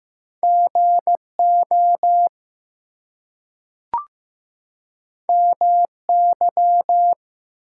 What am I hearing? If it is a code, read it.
Morse code, heard as a keyed tone, about 15 wpm, 700 Hz. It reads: GO MY
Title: Two Letter Words - 15wpm